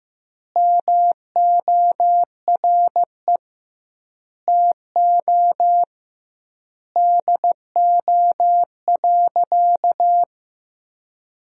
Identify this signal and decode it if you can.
Morse code, heard as a keyed tone, about 15 wpm, 700 Hz. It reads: MORE TO DO.
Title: 2x Word-Spacing - Sentences from Top 100 Words - 15wpm